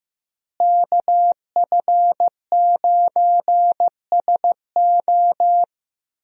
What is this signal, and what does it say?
Morse code, heard as a keyed tone, about 16 wpm, 700 Hz. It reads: KF9SO
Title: USA Call Signs - 5 Characters - Repeated 3x - 15wpm